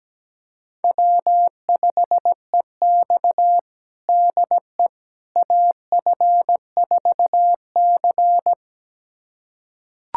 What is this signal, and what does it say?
Morse code, heard as a keyed tone, about 17 wpm, 700 Hz. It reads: W5EX DE AF4C
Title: USA Callsign DE Callsign - 4 Characters - 17wpm